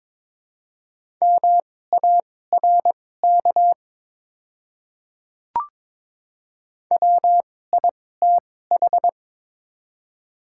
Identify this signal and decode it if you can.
Morse code, heard as a keyed tone, about 23 wpm, spaced as for 16 wpm, 700 Hz. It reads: MARK WITH
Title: 2x Character Spacing - Top 300 Words - 22wpm